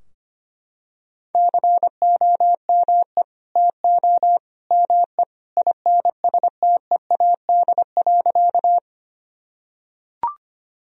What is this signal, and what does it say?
Morse code, heard as a keyed tone, about 25 wpm, 700 Hz. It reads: COME TO ME INSTEAD.
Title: Sentences from Top 1000 Words - Review Part 4 - 25wpm